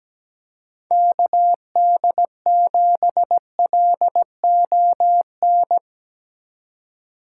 Morse code, heard as a keyed tone, about 17 wpm, 700 Hz. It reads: KD7LON
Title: USA Call Signs - 6 Characters - 3x Character Spacing - 17wpm